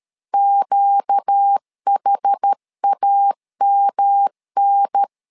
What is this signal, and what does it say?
Morse code, heard as a keyed tone, about 13 wpm, 800 Hz. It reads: QHAMN
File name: LEZIONI / 4 Q1 12wpm 2028